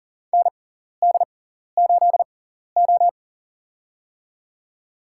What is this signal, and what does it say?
Morse code, heard as a keyed tone, about 40 wpm, spaced as for 14 wpm, 700 Hz. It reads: ND8O
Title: USA Call Signs - 4 Characters - 6x Character Spacing - 40wpm